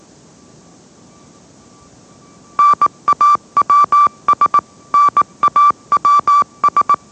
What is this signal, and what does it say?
Morse code, heard as a keyed tone, about 18 wpm, 1160 Hz. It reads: NAWS NAWS
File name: mrx / mrx081 20wpm